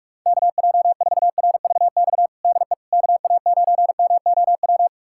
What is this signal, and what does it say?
Morse code, heard as a keyed tone, about 46 wpm, 690 Hz. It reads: KJ4RVX DE KA9MOW
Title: USA Callsign DE Callsign - 6 Characters - 45wpm